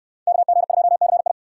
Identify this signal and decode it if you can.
Morse code, heard as a keyed tone, about 61 wpm, 700 Hz. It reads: KG2CI